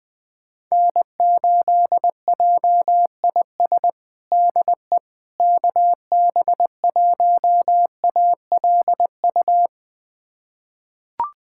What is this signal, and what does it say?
Morse code, heard as a keyed tone, about 20 wpm, 700 Hz. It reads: N8JIS DE KB1ALU
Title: USA Callsign DE Callsign 20wpm